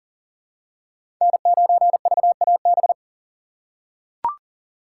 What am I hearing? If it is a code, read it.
Morse code, heard as a keyed tone, about 40 wpm, 700 Hz. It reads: N9VAB
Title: USA Call Signs - 5 Characters - 6x Character Spacing - 40wpm